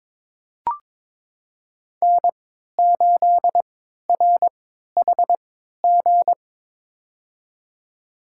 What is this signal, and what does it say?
Morse code, heard as a keyed tone, about 22 wpm, spaced as for 13 wpm, 700 Hz. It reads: N8RHG
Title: USA Call Signs - 5 Characters - 3x Character Spacing - 22wpm